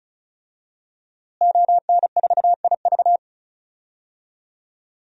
Morse code, heard as a keyed tone, about 36 wpm, 700 Hz. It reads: ON4IV
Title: DXCC-100 Call Signs - 5 Characters - Repeated 3x - 35wpm